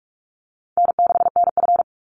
Morse code, heard as a keyed tone, about 44 wpm, 700 Hz. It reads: N6NF
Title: Easy Call Signs Repeated 2x - 45wpm